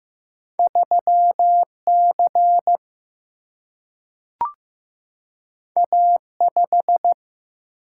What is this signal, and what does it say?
Morse code, heard as a keyed tone, about 15 wpm, 700 Hz. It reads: 3 C A 5
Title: Single-Letter Single-Number 15wpm